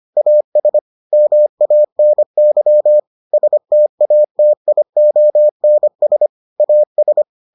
Morse code, heard as a keyed tone, about 25 wpm, 600 Hz. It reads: AS MANY STATIONS AS